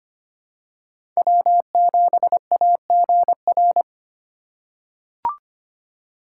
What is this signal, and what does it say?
Morse code, heard as a keyed tone, about 25 wpm, 700 Hz. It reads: W7AGR
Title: USA Call Signs - 5 Characters - 3x Character Spacing - 25wpm